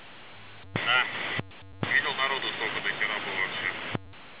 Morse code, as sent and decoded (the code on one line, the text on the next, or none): none